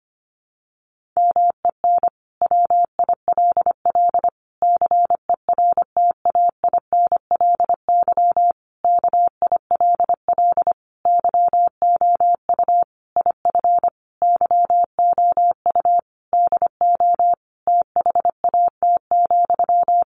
Morse code, heard as a keyed tone, about 25 wpm, 700 Hz. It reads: MEN WILL CERTAINLY KILL YOU IF YOU DO THAT,
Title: Sentences from Top 1500 Words 25wpm